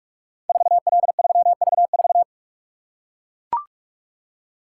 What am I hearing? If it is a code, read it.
Morse code, heard as a keyed tone, about 44 wpm, 690 Hz. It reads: 4L3V4
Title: Code Groups - 5 Letter:Numbers - 45wpm